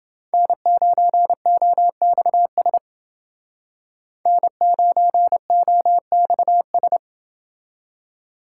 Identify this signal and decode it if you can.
Morse code, heard as a keyed tone, about 29 wpm, 700 Hz. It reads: N9OXS N9OXS